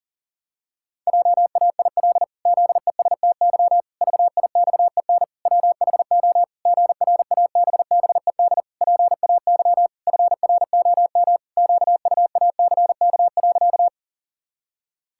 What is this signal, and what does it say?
Morse code, heard as a keyed tone, about 40 wpm, 700 Hz. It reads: JAIL ZESTY VIXEN WHO GRABBED PAY FROM QUACK.
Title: Pangram Challenge v2 - 40wpm